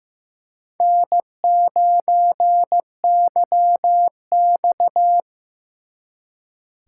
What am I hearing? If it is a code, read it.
Morse code, heard as a keyed tone, about 18 wpm, 700 Hz. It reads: N9YX
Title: Speed Racing - USA Call Signs 15wpm